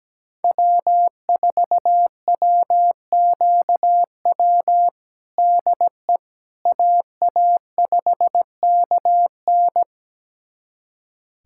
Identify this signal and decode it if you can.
Morse code, heard as a keyed tone, about 17 wpm, 700 Hz. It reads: W4WQW DE AA5KN
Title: USA Callsign DE Callsign - 5 Characters - 17wpm